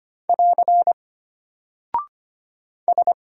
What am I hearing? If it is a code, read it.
Morse code, heard as a keyed tone, about 25 wpm, 700 Hz. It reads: <AR> S
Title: Single Character (Letter:Number:Punctuation:Prosign) Alt 25wpm